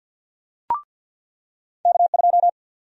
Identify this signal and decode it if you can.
Morse code, heard as a keyed tone, about 50 wpm, 700 Hz. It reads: K2
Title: USA Call Sign Prefixes - 50wpm